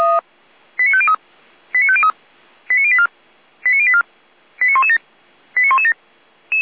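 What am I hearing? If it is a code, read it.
Five-tone selective call (ZVEI), heard as quick runs of five tones, tones of about 70 ms. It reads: 89503 89503 89084 89084 89117 89117
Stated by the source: ZVEI five-tone call received by SDR, fire brigade Orlamünde (feuerwehr-orlamuende).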